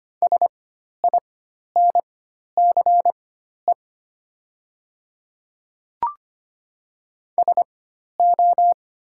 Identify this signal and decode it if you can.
Morse code, heard as a keyed tone, about 25 wpm, spaced as for 12 wpm, 700 Hz. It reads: SINCE SO
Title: 4x Character Spacing - Top 300 Words - 25wpm